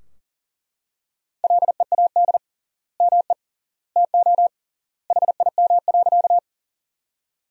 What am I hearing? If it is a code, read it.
Morse code, heard as a keyed tone, about 40 wpm, 700 Hz. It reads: LEAD ME TO HIM.
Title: 3x Word-Spacing - Sentences from Top 500 Words - 40wpm